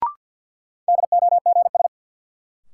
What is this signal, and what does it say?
Morse code, heard as a keyed tone, about 53 wpm, 700 Hz. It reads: DOGS
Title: Sentences from Top 1600 Words - New Word Drill Alt - 50wpm